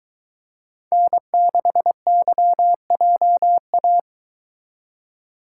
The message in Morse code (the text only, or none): N6YJA